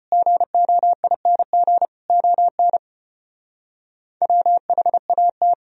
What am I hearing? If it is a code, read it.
Morse code, heard as a keyed tone, about 32 wpm, 700 Hz. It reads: GOING ON WHAT